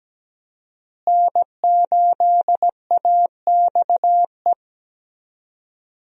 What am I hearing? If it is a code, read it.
Morse code, heard as a keyed tone, about 17 wpm, 700 Hz. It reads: N8AXE
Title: USA Call Signs - 5 Characters - Repeated 3x - 17wpm